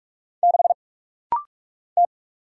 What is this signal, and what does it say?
Morse code, heard as a keyed tone, about 44 wpm, 700 Hz. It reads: 6 T
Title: Single Letter-Number - Rapid-Fire 45wpm